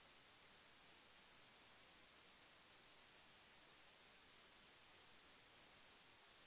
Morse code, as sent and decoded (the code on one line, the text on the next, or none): none